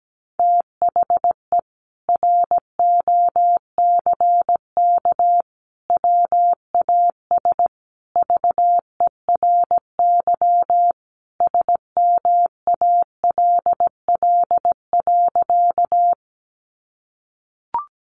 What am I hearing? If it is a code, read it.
Morse code, heard as a keyed tone, about 17 wpm, 700 Hz. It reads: THE ROCK WAS VERY SMALL.
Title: Sentences from Top 400 Words 17wpm